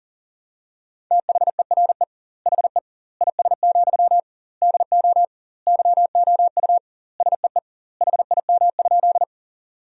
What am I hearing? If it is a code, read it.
Morse code, heard as a keyed tone, about 40 wpm, 700 Hz. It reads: THERE HE IS, DO YOU SEE HIM?
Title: 2x Word-Spacing - Sentences from Top 100 Words - 40wpm